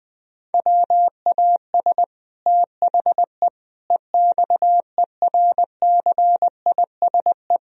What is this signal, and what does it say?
Morse code, heard as a keyed tone, about 21 wpm, 700 Hz. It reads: WAS THE EXERCISE
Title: Sentences from Top 1000 Words - Review Part 4 - 20wpm